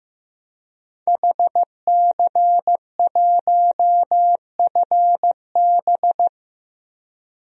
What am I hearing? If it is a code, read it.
Morse code, heard as a keyed tone, about 15 wpm, 700 Hz. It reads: HC1FB
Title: DXCC-100 Call Signs - 5 Characters - 15wpm